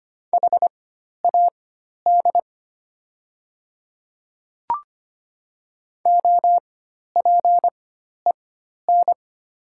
Morse code, heard as a keyed tone, about 25 wpm, spaced as for 12 wpm, 700 Hz. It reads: HAD OPEN
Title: 4x Character Spacing - Top 300 Words - 25wpm